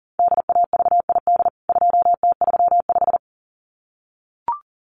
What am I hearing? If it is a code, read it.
Morse code, heard as a keyed tone, about 40 wpm, 700 Hz. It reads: DAVID 2T35
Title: CWT Exchange 40wpm